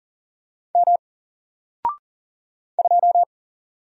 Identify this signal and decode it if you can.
Morse code, heard as a keyed tone, about 39 wpm, 700 Hz. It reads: M 2
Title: Single Letter-Number 40wpm